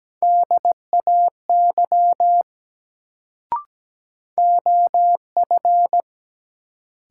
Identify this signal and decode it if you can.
Morse code, heard as a keyed tone, about 17 wpm, 700 Hz. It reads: DAY OF
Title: Sets of 2 Words - New Word Drill - 17wpm